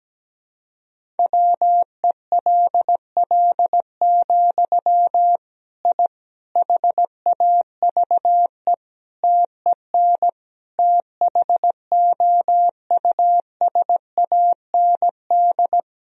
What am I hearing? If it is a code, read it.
Morse code, heard as a keyed tone, about 17 wpm, 700 Hz. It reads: WELL, I HAVE TEN THOUSAND
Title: Sentences from Top 1000 Words - Review Part 3 - 17wpm